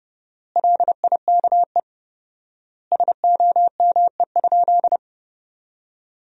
Morse code, heard as a keyed tone, about 32 wpm, 700 Hz. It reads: LIKE SOME?
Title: 4x Word-Spacing - Sentences from Top 100 Words - 30wpm